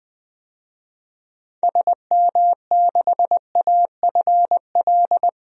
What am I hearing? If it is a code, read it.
Morse code, heard as a keyed tone, about 20 wpm, 700 Hz. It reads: SM6AFL
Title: DXCC-100 Call Signs - 6 Characters - Repeated 3x - 20wpm